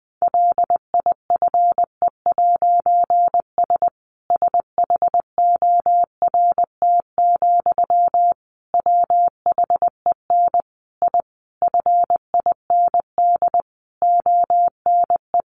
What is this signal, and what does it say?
Morse code, heard as a keyed tone, about 20 wpm, 700 Hz. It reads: LIFE'S SHORT, WHEN I FIND ONE
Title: A Sound Of Thunder 20wpm